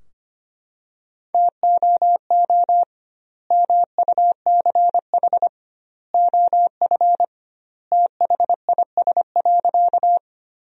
Morse code, heard as a keyed tone, about 25 wpm, 700 Hz. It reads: TOO MUCH OF THIS.
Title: 2x Word-Spacing - Sentences from Top 200 Words - 25wpm